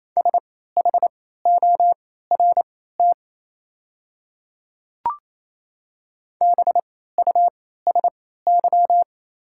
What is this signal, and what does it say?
Morse code, heard as a keyed tone, about 28 wpm, spaced as for 16 wpm, 700 Hz. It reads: SHORT BUSY
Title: 3x Character Spacing - Top 500 Words - 28wpm